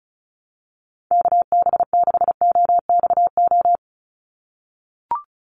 Morse code, heard as a keyed tone, about 34 wpm, 700 Hz. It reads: KB6OXO
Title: USA Call Signs 35wpm